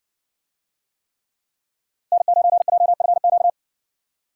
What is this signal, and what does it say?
Morse code, heard as a keyed tone, about 60 wpm, 700 Hz. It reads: N9JFQ